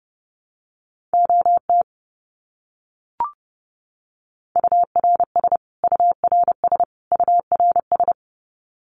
Morse code, heard as a keyed tone, about 30 wpm, 700 Hz. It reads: OT URS URS URS
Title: Abbreviations Repeated 3x - 30wpm